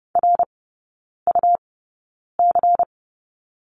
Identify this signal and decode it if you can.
Morse code, heard as a keyed tone, about 30 wpm, spaced as for 9 wpm, 700 Hz. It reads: RUC
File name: Lesson 19e - Letter w - Three Characters 30wpm